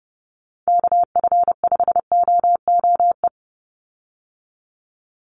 Morse code, heard as a keyed tone, about 30 wpm, 700 Hz. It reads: KF5OOE
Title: USA Call Signs 30wpm